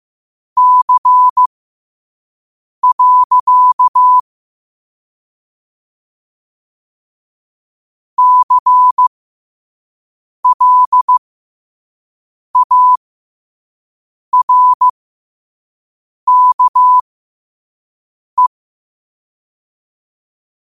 Morse code, heard as a keyed tone, about 15 wpm, spaced as for 5 wpm, 1000 Hz. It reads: C. CLARKE
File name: qotdfiles / 5wpm